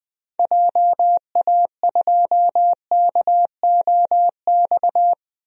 Morse code, heard as a keyed tone, about 20 wpm, 700 Hz. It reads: JA2KOX
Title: DXCC-100 Call Signs - 6 Characters - 20wpm